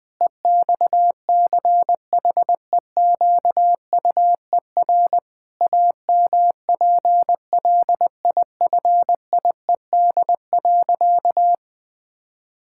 Morse code, heard as a keyed tone, about 20 wpm, 700 Hz. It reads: EXCHEQUER AMPLIFIED.
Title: Pangram Challenge v2 - 20wpm